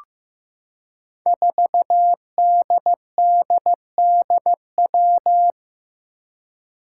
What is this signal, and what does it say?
Morse code, heard as a keyed tone, about 15 wpm, 700 Hz. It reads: 4DDDW
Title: Five Challenging Letter-Number Combinations 15wpm